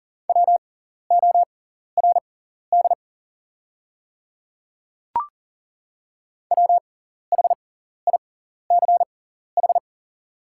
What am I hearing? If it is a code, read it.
Morse code, heard as a keyed tone, about 39 wpm, spaced as for 14 wpm, 700 Hz. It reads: WORD WHICH
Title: 6x Character Spacing - Top 200 Words - 40wpm